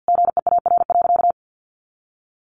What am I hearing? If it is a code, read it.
Morse code, heard as a keyed tone, about 50 wpm, 700 Hz. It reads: GEAR.
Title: Sentences from Top 1400 Words 50wpm